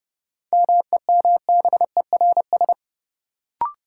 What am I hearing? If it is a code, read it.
Morse code, heard as a keyed tone, about 30 wpm, 700 Hz. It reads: MEMBERS
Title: Sets of 3 Words - New Word Drill - 30wpm